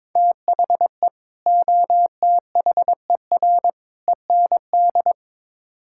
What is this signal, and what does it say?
Morse code, heard as a keyed tone, about 22 wpm, 700 Hz. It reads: THE OTHER END
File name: practice / 22wpm / Build A Phrase - 5-Word Phrases - 22wpm